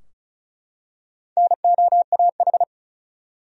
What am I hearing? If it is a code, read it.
Morse code, heard as a keyed tone, about 36 wpm, 700 Hz. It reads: NOAH
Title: USA Common Names - Max 5 Letters - 35wpm